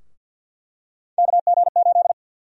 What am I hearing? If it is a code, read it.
Morse code, heard as a keyed tone, about 49 wpm, 700 Hz. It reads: KG8
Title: USA Call Sign Prefixes - Repeated 2x - 50wpm